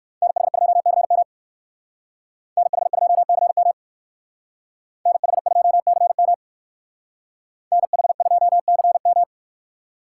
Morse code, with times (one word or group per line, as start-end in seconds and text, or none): 0.2-1.2 NH2CM
2.6-3.7 NH2CM
5.1-6.3 NH2CM
7.7-9.2 NH2CM